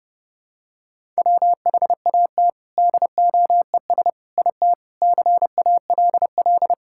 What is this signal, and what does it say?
Morse code, heard as a keyed tone, about 30 wpm, 700 Hz. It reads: WHAT DOES IT CALL
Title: Sentences from Top 1000 Words - Review Part 5 - 30wpm